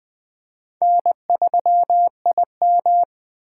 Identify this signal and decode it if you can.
Morse code, heard as a keyed tone, about 20 wpm, 700 Hz. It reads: N3IM
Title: USA Call Signs - 4 Characters - 4x Character Spacing - 20wpm